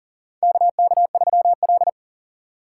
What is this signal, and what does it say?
Morse code, heard as a keyed tone, about 40 wpm, 700 Hz. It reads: KK3L
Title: USA Call Signs - 4 Characters - 4x Character Spacing - 40wpm